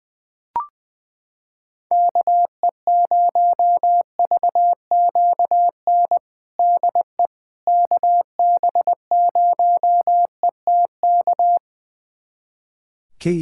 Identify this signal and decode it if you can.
Morse code, heard as a keyed tone, about 20 wpm, 700 Hz. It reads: KE0VQN DE KB0ETK
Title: USA Callsign DE Callsign 20wpm